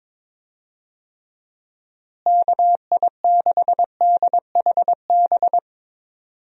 Morse code, heard as a keyed tone, about 22 wpm, 700 Hz. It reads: KI6DHB
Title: USA Call Signs - 6 Characters - Repeated 3x - 22wpm